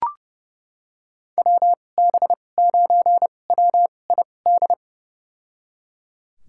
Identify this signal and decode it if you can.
Morse code, heard as a keyed tone, about 30 wpm, spaced as for 22 wpm, 700 Hz. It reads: WB9WID
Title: USA Call Signs - 2x Character Spacing - 30wpm